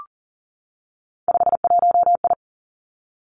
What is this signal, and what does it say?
Morse code, heard as a keyed tone, about 40 wpm, 700 Hz. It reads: 51I
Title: Three Challenging Letter-Number Combinations 40wpm